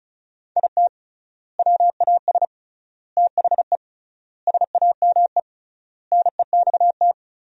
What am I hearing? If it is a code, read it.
Morse code, heard as a keyed tone, about 36 wpm, 700 Hz. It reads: IT WAS THE SAME NEXT
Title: 3x Word-Spacing - Sentences from Top 500 Words - 35wpm